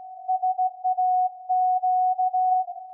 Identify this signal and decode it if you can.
Morse code, heard as a keyed tone, about 16 wpm, 740 Hz. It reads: SAQ
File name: iz7slz / VLF / SAQ 24-12-2021